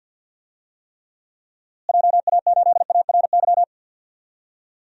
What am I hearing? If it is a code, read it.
Morse code, heard as a keyed tone, about 50 wpm, 700 Hz. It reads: JA8ARY
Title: DXCC-100 Call Signs - 6 Characters - Repeated 2x - 50wpm